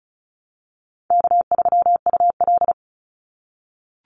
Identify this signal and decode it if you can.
Morse code, heard as a keyed tone, about 36 wpm, 700 Hz. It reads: K3UL
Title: Call Signs - 35wpm